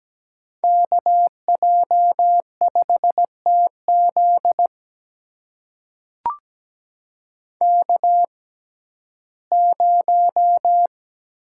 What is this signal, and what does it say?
Morse code, heard as a keyed tone, about 17 wpm, spaced as for 12 wpm, 700 Hz. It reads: KJ5TZ K0
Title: USA Call Signs - 5 Characters - 6x Character Spacing - 17wpm